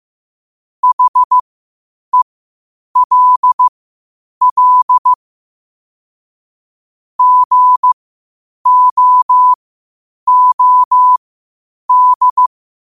Morse code, heard as a keyed tone, about 15 wpm, spaced as for 8 wpm, 1000 Hz. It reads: HELL GOOD
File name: qotdfiles / 8wpm